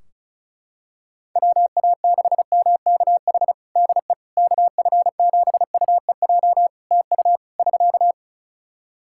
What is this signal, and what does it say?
Morse code, heard as a keyed tone, about 36 wpm, 700 Hz. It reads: WA6MKH DE KF7UEJ TU <SK>
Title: K1USN SST QSO Practice 35wpm